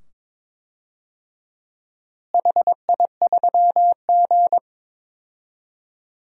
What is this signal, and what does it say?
Morse code, heard as a keyed tone, about 22 wpm, 700 Hz. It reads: HI3G
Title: DXCC-100 Call Signs - 4 Characters - Repeated 2x - 22wpm